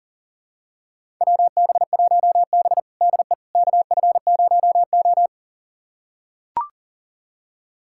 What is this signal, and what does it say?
Morse code, heard as a keyed tone, about 37 wpm, 700 Hz. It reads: WB1B DE KF0O
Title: USA Callsign DE Callsign - 4 Characters - 40wpm